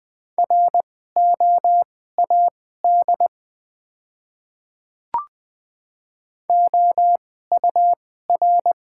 Morse code, heard as a keyed tone, about 20 wpm, spaced as for 14 wpm, 700 Hz. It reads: ROAD OUR
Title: 2x Character Spacing - Top 500 Words - 20wpm